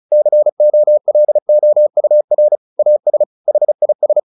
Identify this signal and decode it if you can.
Morse code, heard as a keyed tone, about 34 wpm, 600 Hz. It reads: COLOUR AS HIS
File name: the old man and the sea 2 35wpm